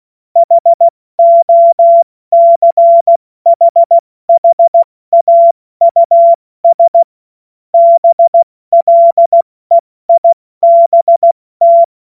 Morse code, heard as a keyed tone, about 16 wpm, 680 Hz. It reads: HOCHHAUS BLEIBT
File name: Morsekurs / Witze / Witz 07.November 2023 70 F80